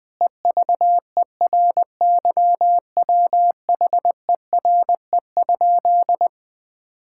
Morse code, heard as a keyed tone, about 20 wpm, 700 Hz. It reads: EVERYWHERE?
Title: Sentences from Top 1600 Words 20wpm